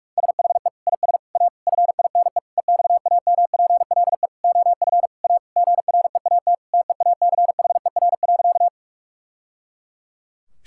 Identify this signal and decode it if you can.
Morse code, heard as a keyed tone, about 46 wpm, 700 Hz. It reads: SHE IS A FINE EXAMPLE OF A GREAT TEACHER.
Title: Sentences from Top 1000 Words - Review Part 6 - 45wpm